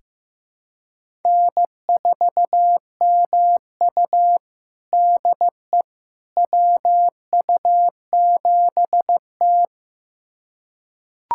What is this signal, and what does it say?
Morse code, heard as a keyed tone, about 15 wpm, 700 Hz. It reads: N4MU DE WU7T
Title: USA Callsign DE Callsign - 4 Characters - 15wpm